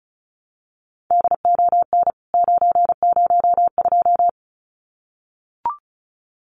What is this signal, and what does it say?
Morse code, heard as a keyed tone, about 34 wpm, 700 Hz. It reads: DON 902
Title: CWT Exchange 35wpm